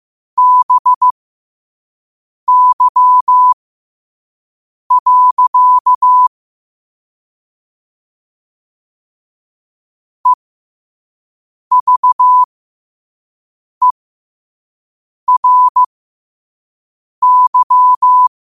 Morse code, heard as a keyed tone, about 15 wpm, spaced as for 5 wpm, 1000 Hz. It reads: BY. EVERY
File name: qotdfiles / 5wpm